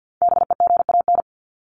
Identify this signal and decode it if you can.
Morse code, heard as a keyed tone, about 49 wpm, 700 Hz. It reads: BEGAN
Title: Top 500 Words Repeated 50wpm